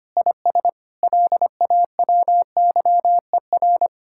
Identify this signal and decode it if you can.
Morse code, heard as a keyed tone, about 24 wpm, 700 Hz. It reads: IS LAWYER
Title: Speed-Racing Common QSO Phrases - 25wpm